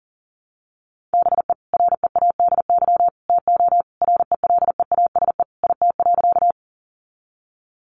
Morse code, heard as a keyed tone, about 40 wpm, 700 Hz. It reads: BE READY TO RELEASE IT.
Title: Sentences from Top 1100 Words 40wpm